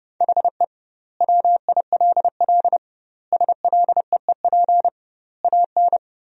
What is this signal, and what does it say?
Morse code, heard as a keyed tone, about 30 wpm, 700 Hz. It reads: HE WILL SLEEP AN
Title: 2x Word-Spacing - Sentences from Top 500 Words - 30wpm